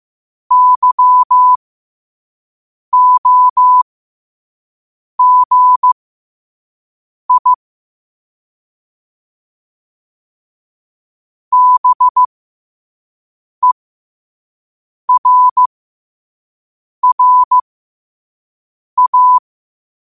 Morse code, heard as a keyed tone, about 15 wpm, spaced as for 5 wpm, 1000 Hz. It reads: YOGI BERRA
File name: qotdfiles / 5wpm